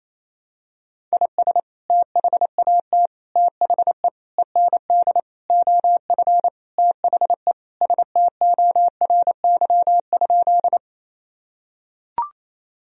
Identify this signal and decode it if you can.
Morse code, heard as a keyed tone, about 28 wpm, 700 Hz. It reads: IS THAT THE END OF THE STORY?
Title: Sentences from Top 1000 Words - Review Part 1 - 28wpm